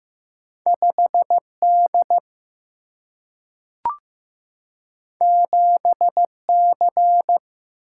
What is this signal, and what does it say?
Morse code, heard as a keyed tone, about 15 wpm, 700 Hz. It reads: 5D 7C
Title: Single-Letter Single-Number 15wpm